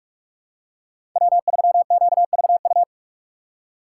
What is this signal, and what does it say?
Morse code, heard as a keyed tone, about 46 wpm, 690 Hz. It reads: W3QVU
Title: USA Call Signs - 5 Characters - 6x Character Spacing - 45wpm